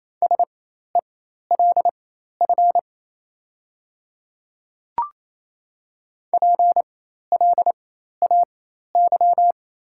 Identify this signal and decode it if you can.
Morse code, heard as a keyed tone, about 28 wpm, spaced as for 13 wpm, 690 Hz. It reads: SELF PLAY
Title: 4x Character Spacing - Top 200 Words - 28wpm